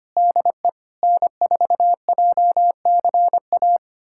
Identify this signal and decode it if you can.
Morse code, heard as a keyed tone, about 25 wpm, 700 Hz. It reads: DE N4JCA
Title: USA Callsign DE Callsign - 5 Characters - 25wpm